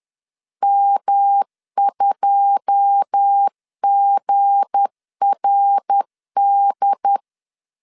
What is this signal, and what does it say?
Morse code, heard as a keyed tone, about 10 wpm, 800 Hz. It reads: M2GRD
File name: LEZIONI / 9 2 10wpm 2736